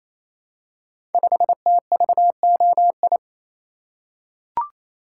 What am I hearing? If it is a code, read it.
Morse code, heard as a keyed tone, about 27 wpm, 700 Hz. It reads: 5TVOI
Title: Code Groups - 5 Letter:Numbers - 28wpm